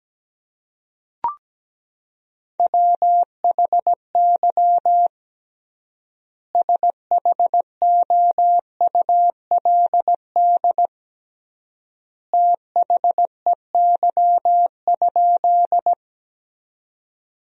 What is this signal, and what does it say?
Morse code, heard as a keyed tone, about 17 wpm, 700 Hz. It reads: WHY SHOULD THEY?
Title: 3x Word-Spacing - Sentences from Top 300 Words - 17wpm